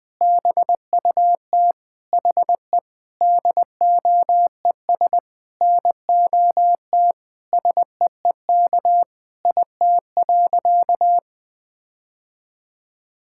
Morse code, heard as a keyed tone, about 20 wpm, 700 Hz. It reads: BUT HE DOES NOT SEEK IT.
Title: Sentences from Top 1600 Words 20wpm